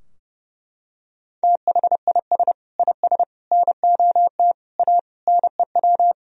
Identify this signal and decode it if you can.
Morse code, heard as a keyed tone, about 32 wpm, 700 Hz. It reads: THIS IS NOT A NEW
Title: Sentences from Top 1000 Words - Review Part 4 - 30wpm